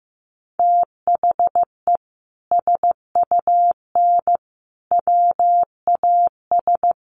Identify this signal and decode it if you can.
Morse code, heard as a keyed tone, about 15 wpm, 700 Hz. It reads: THE SUN WAS
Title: Sentences from Top 1300 Words 15wpm